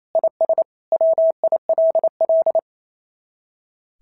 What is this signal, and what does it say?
Morse code, heard as a keyed tone, about 25 wpm, 650 Hz. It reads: IS WILL
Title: SOTA QSO Practice 28wpm